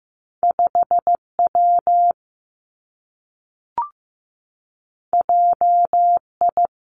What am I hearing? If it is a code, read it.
Morse code, heard as a keyed tone, about 15 wpm, 700 Hz. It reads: 5 W J I
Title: Two Challenging Letter-Number Combinations 15wpm